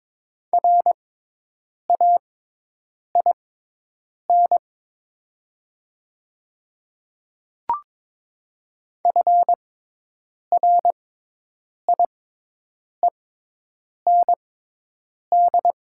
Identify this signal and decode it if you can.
Morse code, heard as a keyed tone, about 22 wpm, spaced as for 8 wpm, 700 Hz. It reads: RAIN FRIEND
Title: 6x Character Spacing - Top 300 Words - 22wpm